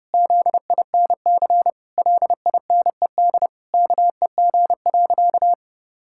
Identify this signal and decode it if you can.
Morse code, heard as a keyed tone, about 30 wpm, 700 Hz. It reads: ZINC LINED KEG.
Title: Pangram Challenge v2 - 30wpm